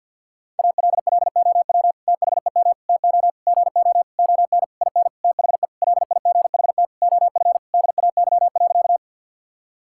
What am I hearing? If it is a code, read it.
Morse code, heard as a keyed tone, about 50 wpm, 700 Hz. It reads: ALLOW THEM TO GO ON IN THE LIGHT OF DAY.
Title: Sentences from Top 1000 Words - Review Part 5 - 50wpm